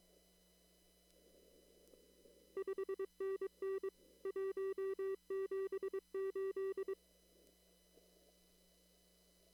Morse code, heard as a keyed tone, about 22 wpm, 390 Hz. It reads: 5NN 178
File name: CW / kt5j wpx cw-op-k5pi